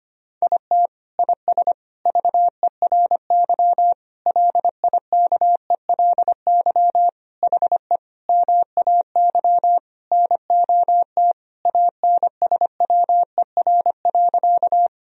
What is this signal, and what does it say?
Morse code, heard as a keyed tone, about 25 wpm, 700 Hz. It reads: IT IS VERY LIKELY HE MAY NOT ANSWER.
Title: Sentences from Top 1000 Words - Review Part 4 - 25wpm